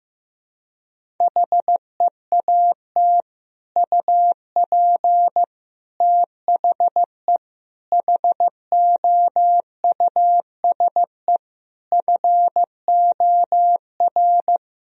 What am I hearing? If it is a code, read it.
Morse code, heard as a keyed tone, about 15 wpm, 700 Hz. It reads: HEAT UP THE HOUSE FOR
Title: Sentences from Top 1000 Words - Review Part 3 - 15wpm